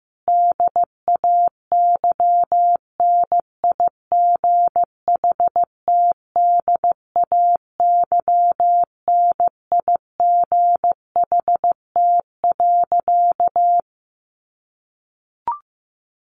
Morse code, heard as a keyed tone, about 15 wpm, 700 Hz. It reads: DAYNIGHTDAYNIGHT.
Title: A Sound Of Thunder 15wpm